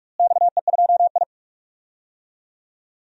Morse code, heard as a keyed tone, about 46 wpm, 700 Hz. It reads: XE2I
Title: DXCC-100 Call Signs - 4 Characters - Repeated 2x - 45wpm